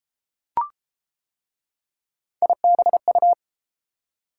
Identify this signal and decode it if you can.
Morse code, heard as a keyed tone, about 30 wpm, 700 Hz. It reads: IBU